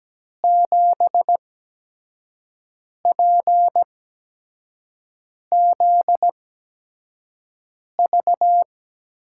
Morse code, heard as a keyed tone, about 17 wpm, spaced as for 5 wpm, 700 Hz. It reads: 7PZV
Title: USA Call Signs - 6 Characters - 8x Character Spacing - 17wpm